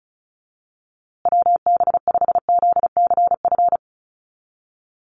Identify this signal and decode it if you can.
Morse code, heard as a keyed tone, about 34 wpm, 700 Hz. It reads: WB5ZCF
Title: USA Call Signs Repeated 3x 35wpm